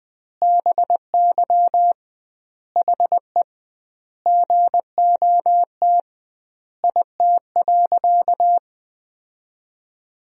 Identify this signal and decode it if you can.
Morse code, heard as a keyed tone, about 20 wpm, 700 Hz. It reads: BY HE GOT IT.
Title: 2x Word-Spacing - Sentences from Top 300 Words - 20wpm